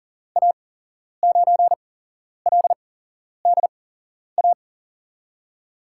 Morse code, heard as a keyed tone, about 39 wpm, spaced as for 11 wpm, 700 Hz. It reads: A9LDA
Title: USA Call Signs - 6 Characters - 8x Character Spacing - 40wpm